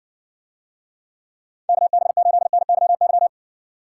Morse code, heard as a keyed tone, about 60 wpm, 700 Hz. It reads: KB8NYY